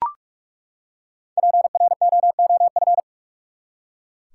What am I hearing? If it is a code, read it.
Morse code, heard as a keyed tone, about 42 wpm, 700 Hz. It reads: PROOF
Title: Five Letter Words - 45wpm